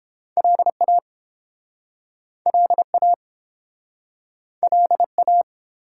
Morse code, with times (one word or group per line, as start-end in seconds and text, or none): 0.4-1.0 LA
2.5-3.1 LA
4.6-5.4 LA